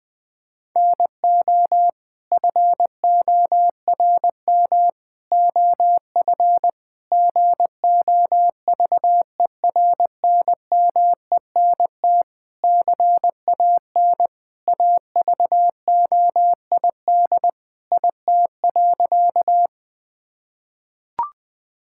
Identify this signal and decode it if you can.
Morse code, heard as a keyed tone, about 21 wpm, 700 Hz. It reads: NO FORM OF GOVERNMENT CAN AVOID IT.
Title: Sentences from Top 1000 Words - Review Part 6 - 20wpm